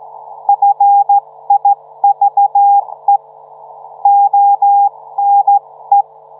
Morse code, heard as a keyed tone, about 15 wpm, 810 Hz. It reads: FIVE ONE